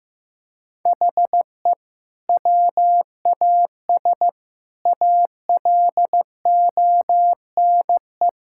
Morse code, heard as a keyed tone, about 15 wpm, 700 Hz. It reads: HE WAS ALONE
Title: Sentences from Top 1000 Words - Review Part 6 - 15wpm